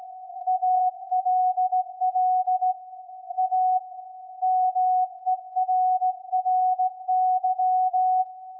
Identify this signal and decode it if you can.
Morse code, heard as a keyed tone, about 16 wpm, 740 Hz. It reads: ALL A MERRY
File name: iz7slz / VLF / SAQ 24-12-2021